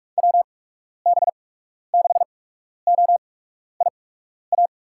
Morse code, heard as a keyed tone, about 46 wpm, spaced as for 12 wpm, 700 Hz. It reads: WB6OIA
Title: USA Call Signs - 8x Character Spacing - 45wpm